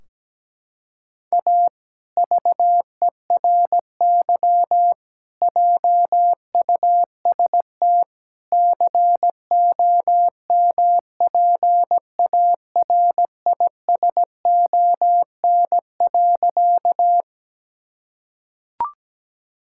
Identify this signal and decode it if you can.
Morse code, heard as a keyed tone, about 17 wpm, 700 Hz. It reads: A VERY JUST COMPARISON.
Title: Sentences from Top 1600 Words 17wpm